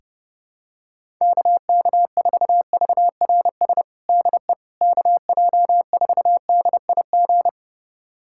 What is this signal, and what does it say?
Morse code, heard as a keyed tone, about 30 wpm, 700 Hz. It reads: KK4VRS DE KJ4DIG
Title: USA Callsign DE Callsign - 6 Characters - 30wpm